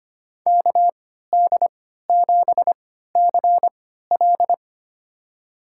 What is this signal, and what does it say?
Morse code, heard as a keyed tone, about 25 wpm, spaced as for 14 wpm, 700 Hz. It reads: KD7CL
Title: USA Call Signs - 5 Characters - 3x Character Spacing - 25wpm